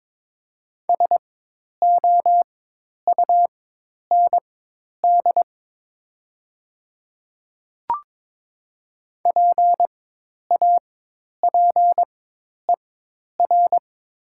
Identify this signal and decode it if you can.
Morse code, heard as a keyed tone, about 22 wpm, spaced as for 10 wpm, 700 Hz. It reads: SOUND PAPER
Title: 4x Character Spacing - Top 300 Words - 22wpm